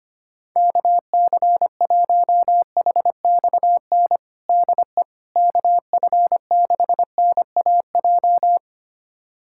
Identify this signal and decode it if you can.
Morse code, heard as a keyed tone, about 25 wpm, 700 Hz. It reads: KC1HXN DE KF6NAJ
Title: USA Callsign DE Callsign - 6 Characters - 25wpm